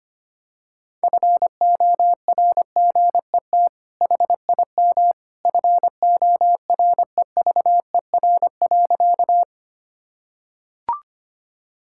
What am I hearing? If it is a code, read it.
Morse code, heard as a keyed tone, about 25 wpm, 700 Hz. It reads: FORGET HIM FOREVER.
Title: Sentences from Top 1600 Words 25wpm